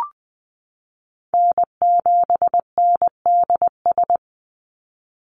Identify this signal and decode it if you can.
Morse code, heard as a keyed tone, about 20 wpm, 700 Hz. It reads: N7NDS
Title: USA Call Signs 20wpm